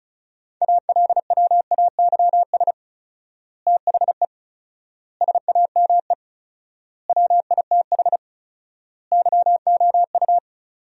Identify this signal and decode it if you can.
Morse code, heard as a keyed tone, about 34 wpm, 700 Hz. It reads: ALWAYS THE SAME WITH YOU
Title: 4x Word-Spacing - Sentences from Top 300 Words - 35wpm